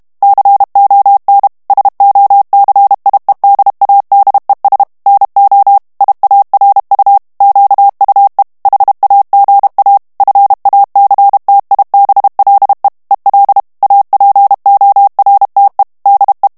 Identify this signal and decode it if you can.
Morse code, heard as a keyed tone, about 32 wpm, 800 Hz. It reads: CON SOCIEDADES NO IARU QUE HAGA FACTIBLE EL APORTE DE
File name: morse mp3 / ESPAGNOL TEXTE EN CLAIR 1 VITESSE 30 MOTS MINUTES